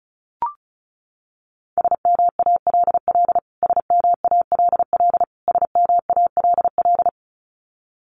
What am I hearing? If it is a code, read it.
Morse code, heard as a keyed tone, about 34 wpm, 700 Hz. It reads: SMALL SMALL SMALL
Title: Top 500 Words Repeated 35wpm